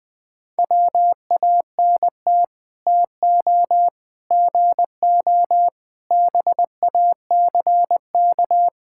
Morse code, heard as a keyed tone, about 20 wpm, 700 Hz. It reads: WANT TO GO BACK
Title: Build A Phrase - 5-Word Phrases - 20wpm